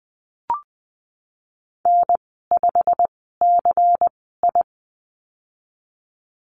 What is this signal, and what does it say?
Morse code, heard as a keyed tone, about 19 wpm, spaced as for 14 wpm, 700 Hz. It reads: N5CI
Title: 173 Lesson 28g - Number 9 - Call Signs 20wpm